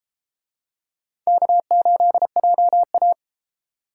Morse code, heard as a keyed tone, about 33 wpm, 700 Hz. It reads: K8JA